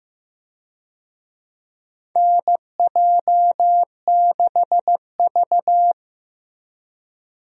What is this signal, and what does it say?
Morse code, heard as a keyed tone, about 15 wpm, 700 Hz. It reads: NJ6V
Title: USA Call Signs - 4 Characters - Repeated 2x - 15wpm